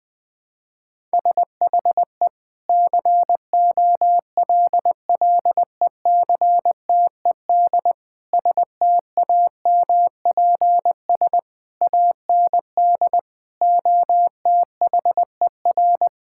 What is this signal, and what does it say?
Morse code, heard as a keyed tone, about 21 wpm, 700 Hz. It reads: SHE COLLECTED STAMPS AND OTHER
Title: Sentences with 2-Word Phrases - Spoken Beginning - 20wpm